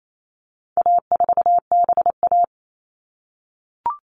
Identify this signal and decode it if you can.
Morse code, heard as a keyed tone, about 28 wpm, 700 Hz. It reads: A4BA
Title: All Cut Numbers - 4-Digit 28wpm